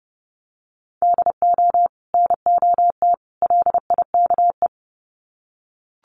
Morse code, heard as a keyed tone, about 30 wpm, 700 Hz. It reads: DO NOT LIKE
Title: Sets of 3 Words 30wpm